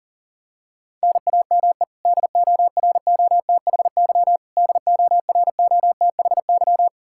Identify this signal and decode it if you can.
Morse code, heard as a keyed tone, about 40 wpm, 700 Hz. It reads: NAME DOROTHY DOROTHY
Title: Practice QSO - Exchange 1 - Beginning - 40wpm